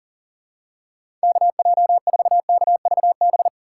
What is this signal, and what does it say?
Morse code, heard as a keyed tone, about 40 wpm, 700 Hz. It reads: KJ4KVB
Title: USA Call Signs - 2x Character Spacing - 40wpm